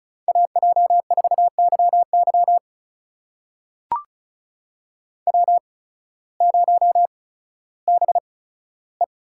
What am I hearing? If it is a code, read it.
Morse code, heard as a keyed tone, about 36 wpm, spaced as for 16 wpm, 700 Hz. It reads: AJ4YY W0BE
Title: USA Call Signs - 8x Character Spacing - 35wpm